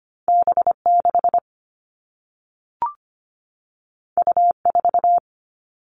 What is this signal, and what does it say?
Morse code, heard as a keyed tone, about 25 wpm, 700 Hz. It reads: B6 U4
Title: All Cut Numbers - 2-Digit 25wpm